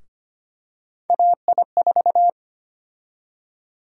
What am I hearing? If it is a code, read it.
Morse code, heard as a keyed tone, about 25 wpm, 700 Hz. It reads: AI4
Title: USA Call Sign Prefixes - Repeated 2x - 25wpm